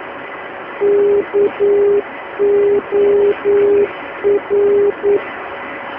Morse code, heard as a keyed tone, about 9 wpm, 410 Hz. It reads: KOR
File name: uploader / 322 khz